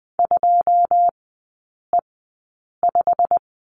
Morse code, heard as a keyed tone, about 20 wpm, spaced as for 8 wpm, 700 Hz. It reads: 2E5
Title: 087 Lesson 15e - Number 5 - Three Characters 20wpm